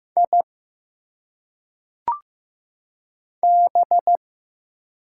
Morse code, heard as a keyed tone, about 15 wpm, 700 Hz. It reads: I B
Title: Single Letters - Phonetic Pronunciation 15wpm